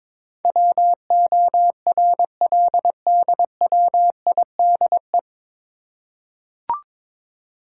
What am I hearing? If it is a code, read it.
Morse code, heard as a keyed tone, about 23 wpm, 700 Hz. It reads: WORLDWIDE
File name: 9 Letter Words - First 3 Letters Given - 22wpm